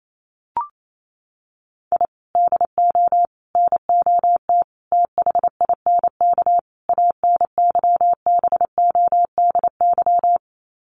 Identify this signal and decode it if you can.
Morse code, heard as a keyed tone, about 28 wpm, 700 Hz. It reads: I DO NOT THINK ANYBODY
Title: Sets of 5 Words 28wpm